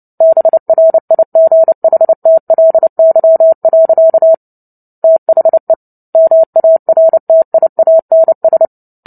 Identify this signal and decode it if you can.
Morse code, heard as a keyed tone, about 30 wpm, 650 Hz. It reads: BRIGHTLY. THE MARTIANS
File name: Book Two / 03-Chapter 3 - 29 WPM